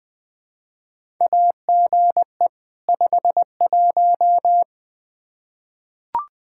Morse code, heard as a keyed tone, about 20 wpm, 700 Hz. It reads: AGE 51
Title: Practice QSO - Exchange 3 - More Info - 20wpm